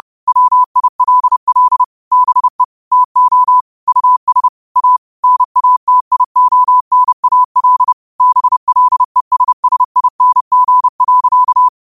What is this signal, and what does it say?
Morse code, heard as a keyed tone, about 30 wpm, 1000 Hz. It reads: WILL BE TO US A NATIONAL BLESSING.
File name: 30wpm